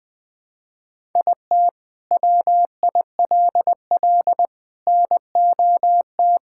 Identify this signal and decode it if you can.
Morse code, heard as a keyed tone, about 20 wpm, 700 Hz. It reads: IT WILL NOT
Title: Sentences from Top 1000 Words - Review Part 6 - 20wpm